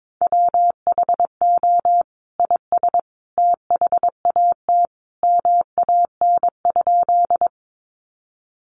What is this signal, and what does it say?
Morse code, heard as a keyed tone, about 22 wpm, 700 Hz. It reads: WHO IS THAT MAN?
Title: Sentences from Top 200 Words 22wpm